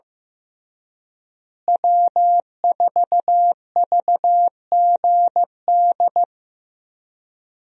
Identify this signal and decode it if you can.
Morse code, heard as a keyed tone, about 15 wpm, 700 Hz. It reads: W4VGD
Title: USA Call Signs - 5 Characters - Repeated 3x - 15wpm